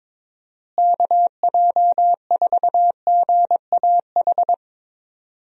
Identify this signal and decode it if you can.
Morse code, heard as a keyed tone, about 22 wpm, 700 Hz. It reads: KJ4GAH
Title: USA Call Signs - 6 Characters - Repeated 3x - 22wpm